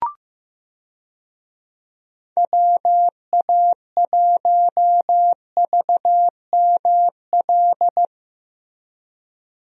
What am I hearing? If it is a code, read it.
Morse code, heard as a keyed tone, about 15 wpm, 700 Hz. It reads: WA1VML
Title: USA Call Signs - 6 Characters - Repeated 3x - 15wpm